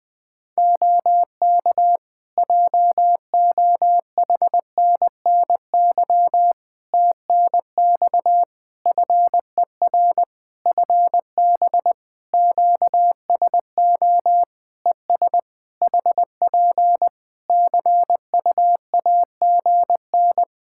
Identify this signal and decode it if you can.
Morse code, heard as a keyed tone, about 21 wpm, 700 Hz. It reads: OK JOHNNY TNX FER FB QSO ES HP CUAGN
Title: Practice QSO - Exchange 4 - Ending - 20wpm